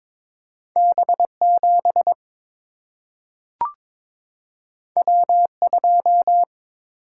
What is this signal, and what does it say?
Morse code, heard as a keyed tone, about 22 wpm, 700 Hz. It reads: B7 W2
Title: Single-Letter Single-Number 22wpm